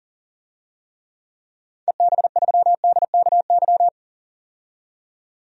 Morse code, heard as a keyed tone, about 40 wpm, 700 Hz. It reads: EB3DKY
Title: DXCC-100 Call Signs - 6 Characters - 40wpm